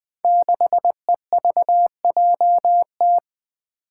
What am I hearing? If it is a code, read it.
Morse code, heard as a keyed tone, about 20 wpm, 700 Hz. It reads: 6EVJT
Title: Five Challenging Letter-Number Combinations 20wpm